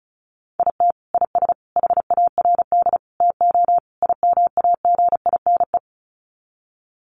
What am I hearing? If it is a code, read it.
Morse code, heard as a keyed tone, about 34 wpm, 700 Hz. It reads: IT IS HARD TO IMAGINE
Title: Sets of 5 Words Encore2 35wpm